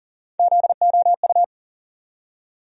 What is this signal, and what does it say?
Morse code, heard as a keyed tone, about 40 wpm, 700 Hz. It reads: ZOU
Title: USA Call Sign Postfixes - 3 Letters 40wpm